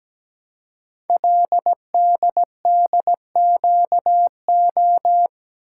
Morse code, heard as a keyed tone, about 17 wpm, 700 Hz. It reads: LDDQO
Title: Code Groups - 5 Letters - 17wpm